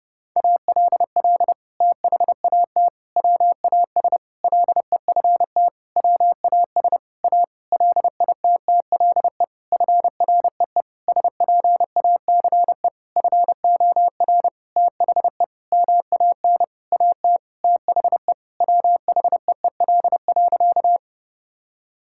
Morse code, heard as a keyed tone, about 30 wpm, 700 Hz. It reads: ALL THAT WAS LEFT WAS A LITTLE FREE SPACE FOR THE MAN AT THE WHEEL.
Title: Sentences from Top 1000 Words - Review Part 5 - 30wpm